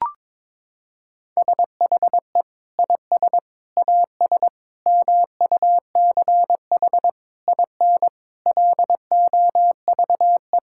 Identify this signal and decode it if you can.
Morse code, heard as a keyed tone, about 22 wpm, 700 Hz. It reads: SHE IS AS MUCH IN LOVE
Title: Sentences from Top 1000 Words - Review Part 5 - 22wpm